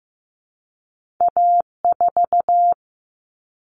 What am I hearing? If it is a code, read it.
Morse code, heard as a keyed tone, about 15 wpm, 700 Hz. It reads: A4
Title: All Cut Numbers - 2-Digit 15wpm